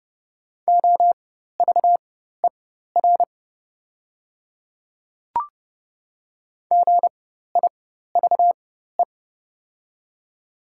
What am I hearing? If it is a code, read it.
Morse code, heard as a keyed tone, about 29 wpm, spaced as for 14 wpm, 690 Hz. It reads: OVER GIVE
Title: 4x Character Spacing - Top 200 Words - 30wpm